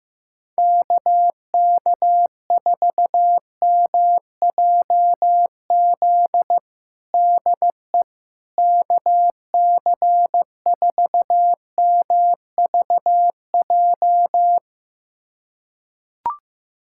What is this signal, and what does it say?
Morse code, heard as a keyed tone, about 16 wpm, 700 Hz. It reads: KK4MJZ DE KC4MVJ
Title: USA Callsign DE Callsign - 6 Characters - 15wpm